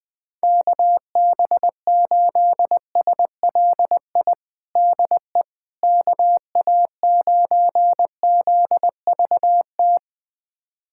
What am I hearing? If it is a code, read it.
Morse code, heard as a keyed tone, about 21 wpm, 700 Hz. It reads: KB8SLI DE KA9ZVT
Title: USA Callsign DE Callsign - 6 Characters - 20wpm